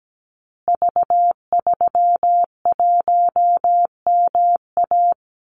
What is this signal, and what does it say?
Morse code, heard as a keyed tone, about 17 wpm, 700 Hz. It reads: V31MA
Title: Call Signs Repeated 3x - 17wpm